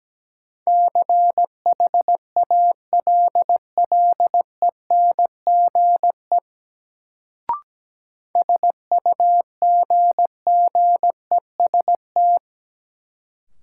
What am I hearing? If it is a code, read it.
Morse code, heard as a keyed tone, about 17 wpm, 700 Hz. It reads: CHALLENGE SUGGEST
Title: Sentences from Top 1300 Words - New Word Drill Alt - 17wpm